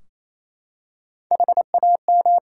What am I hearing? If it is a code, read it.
Morse code, heard as a keyed tone, about 28 wpm, 700 Hz. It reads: HAM
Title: Practice QSO - Exchange 3 - More Info - 28wpm